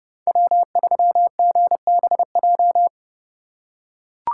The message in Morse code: W3GBJ

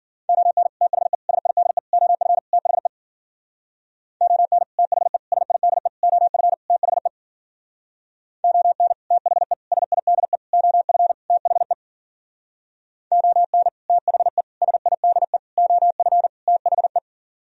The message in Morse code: ON THE SIDE OF THE ON THE SIDE OF THE ON THE SIDE OF THE ON THE SIDE OF THE

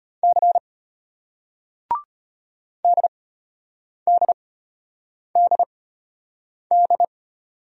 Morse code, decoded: C DDDD

